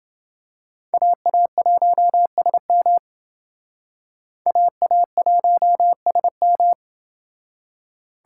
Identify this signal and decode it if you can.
Morse code, heard as a keyed tone, about 29 wpm, 700 Hz. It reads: AA1SM AA1SM